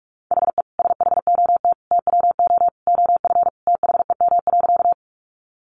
Morse code, heard as a keyed tone, about 44 wpm, 700 Hz. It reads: HE SHOT TWO OF THEM.
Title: Sentences from Top 1100 Words 45wpm